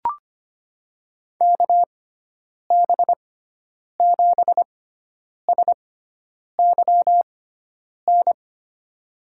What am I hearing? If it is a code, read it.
Morse code, heard as a keyed tone, about 25 wpm, spaced as for 9 wpm, 700 Hz. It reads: KB7SYN
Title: USA Call Signs - 6 Characters - 6x Character Spacing - 25wpm